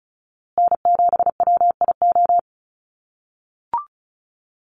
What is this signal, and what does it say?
Morse code, heard as a keyed tone, about 34 wpm, 700 Hz. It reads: N7WIO
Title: USA Call Signs Repeated 2x 35wpm